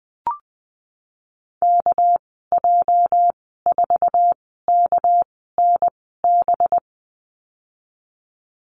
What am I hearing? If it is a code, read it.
Morse code, heard as a keyed tone, about 20 wpm, spaced as for 15 wpm, 700 Hz. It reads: KJ4KNB
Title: 207 Lesson 33g - Letter j - Call Signs 20wpm